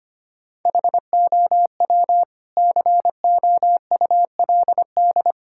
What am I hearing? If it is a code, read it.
Morse code, heard as a keyed tone, about 25 wpm, 700 Hz. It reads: HOW COULD
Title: Sentences from Top 1000 Words - Review Part 6 - 25wpm